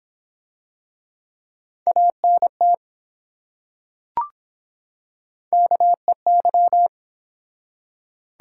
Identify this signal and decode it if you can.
Morse code, heard as a keyed tone, about 23 wpm, 700 Hz. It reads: ANT KEY